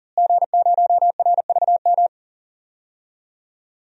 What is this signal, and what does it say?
Morse code, heard as a keyed tone, about 40 wpm, 700 Hz. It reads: G0RVM
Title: DXCC-100 Call Signs - 5 Characters - Repeated 3x - 40wpm